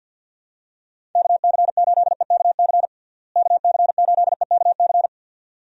Morse code, heard as a keyed tone, about 50 wpm, 700 Hz. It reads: KC8EKC KC8EKC
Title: USA Call Signs - 6 Characters - Repeated 2x - 50wpm